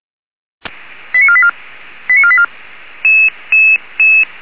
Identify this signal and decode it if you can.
Five-tone selective call (ZVEI), heard as quick runs of five tones, tones of about 70 ms. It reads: 89484 89484